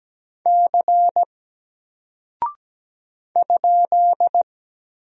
Morse code, heard as a keyed tone, about 17 wpm, 700 Hz. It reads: C ?
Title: Single Character (Letter:Number:Punctuation:Prosign) 17wpm